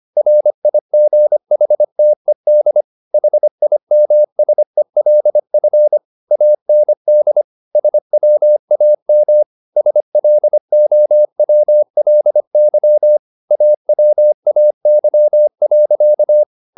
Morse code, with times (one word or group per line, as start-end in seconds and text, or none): 0.2-2.8 RIGHTED
3.1-6.0 HIMSELF
6.3-7.4 AND
7.8-9.4 SWAM
9.8-13.2 SLOWLY
13.5-16.4 AWAY.